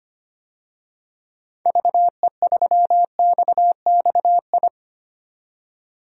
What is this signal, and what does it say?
Morse code, heard as a keyed tone, about 25 wpm, 700 Hz. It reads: VE3XXI
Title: DXCC-100 Call Signs - 6 Characters - 25wpm